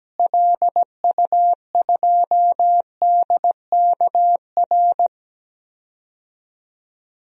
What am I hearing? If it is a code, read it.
Morse code, heard as a keyed tone, about 17 wpm, 700 Hz. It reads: LU2DKR
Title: DXCC-100 Call Signs - 6 Characters - Repeated 2x - 17wpm